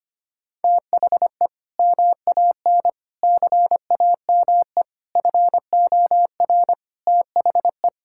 Morse code, heard as a keyed tone, about 25 wpm, 700 Hz. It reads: THE MAN CAME FOR THE
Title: Sentences from Top 1000 Words - Review Part 5 - 25wpm